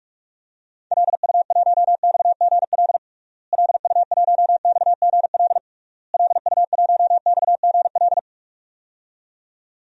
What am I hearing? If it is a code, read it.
Morse code, heard as a keyed tone, about 46 wpm, 700 Hz. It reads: LU1XGL LU1XGL LU1XGL
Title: DXCC-100 Call Signs - 6 Characters - Repeated 3x - 45wpm